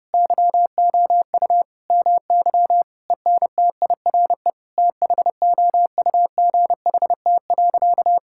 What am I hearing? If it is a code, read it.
Morse code, heard as a keyed tone, about 32 wpm, 700 Hz. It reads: YOU MY ENTIRE THOUGHT.
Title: Sentences from Top 1000 Words - Review Part 5 - 30wpm